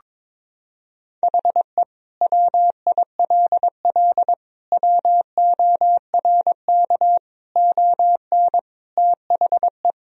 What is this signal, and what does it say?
Morse code, heard as a keyed tone, about 21 wpm, 700 Hz. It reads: HE WILL WORK ON THE
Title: Sentences from Top 1000 Words - Review Part 1 - 22wpm